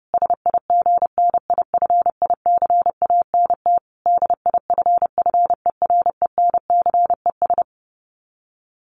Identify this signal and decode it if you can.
Morse code, heard as a keyed tone, about 30 wpm, 700 Hz. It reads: SIGNIFICANT DIFFERENCES
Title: Sets of 5 Words Encore2 30wpm